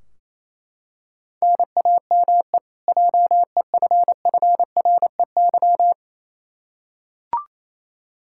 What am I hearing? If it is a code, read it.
Morse code, heard as a keyed tone, about 28 wpm, 700 Hz. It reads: NAME JEFFREY
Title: Practice QSO - Exchange 1 - Beginning - 28wpm